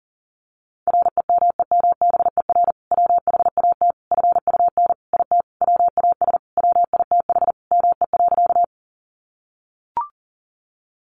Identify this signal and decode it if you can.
Morse code, heard as a keyed tone, about 40 wpm, 700 Hz. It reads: REMEMBER WHAT FUN IT WAS WITH ME.
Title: Sentences from Top 700 Words 40wpm